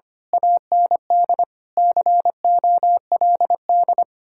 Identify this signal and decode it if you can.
Morse code, heard as a keyed tone, about 25 wpm, 700 Hz. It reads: AND COLD
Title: Speed-Racing Common QSO Phrases - 25wpm